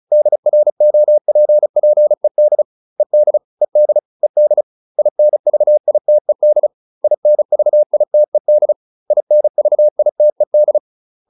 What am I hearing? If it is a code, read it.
Morse code, heard as a keyed tone, about 34 wpm, 600 Hz. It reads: DROPPED ED ED ED INVITED INVITED INVITED